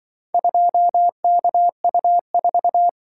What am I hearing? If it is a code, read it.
Morse code, heard as a keyed tone, about 24 wpm, 700 Hz. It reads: 2KU4
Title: Speed-Racing - Four Letter-Number Combinations - 20wpm